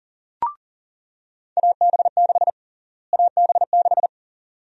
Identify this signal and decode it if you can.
Morse code, heard as a keyed tone, about 37 wpm, 700 Hz. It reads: AB6 AB6
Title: USA Call Sign Prefixes - Repeated 2x - 40wpm